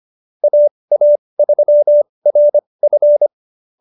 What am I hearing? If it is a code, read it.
Morse code, heard as a keyed tone, about 25 wpm, spaced as for 20 wpm, 590 Hz. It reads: AA3RF